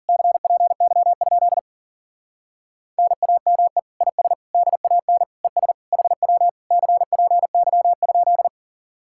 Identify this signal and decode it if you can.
Morse code, heard as a keyed tone, about 49 wpm, 700 Hz. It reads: CPY? NAME IS DAN ES HW CPY?
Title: Speed-Racing Common QSO Phrases Encore - 40wpm